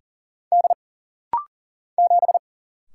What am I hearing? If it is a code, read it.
Morse code, heard as a keyed tone, about 40 wpm, 700 Hz. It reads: D 7
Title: Single Letter-Number - Rapid-Fire 40wpm